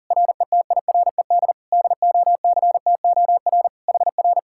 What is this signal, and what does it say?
Morse code, heard as a keyed tone, about 48 wpm, 700 Hz. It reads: RETIRED DOCTOR HR
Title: Speed-Racing Common QSO Phrases Encore - 40wpm